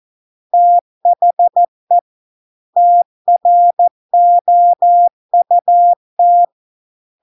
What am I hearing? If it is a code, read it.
Morse code, heard as a keyed tone, about 14 wpm, 700 Hz. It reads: THE TROUT